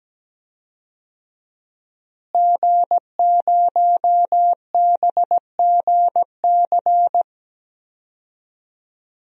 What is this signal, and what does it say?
Morse code, heard as a keyed tone, about 17 wpm, 700 Hz. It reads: G0BGC
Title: DXCC-100 Call Signs - 5 Characters - Repeated 2x - 17wpm